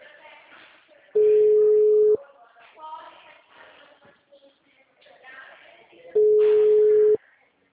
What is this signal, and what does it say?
Ringback tone (Europe, 425 Hz), with speech.
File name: out- -2001- - - .40